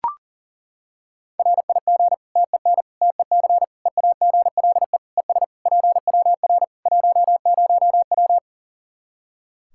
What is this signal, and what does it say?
Morse code, heard as a keyed tone, about 40 wpm, 700 Hz. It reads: RIG TEN TEC EAGLE ES PWR 10W
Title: K1USN SST QSO Practice 40wpm